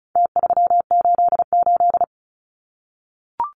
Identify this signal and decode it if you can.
Morse code, heard as a keyed tone, about 34 wpm, 700 Hz. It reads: T388
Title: Numbers - Contest Style - 4-Digit 35wpm